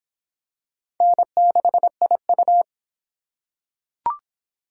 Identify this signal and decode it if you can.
Morse code, heard as a keyed tone, about 23 wpm, 700 Hz. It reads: N6IU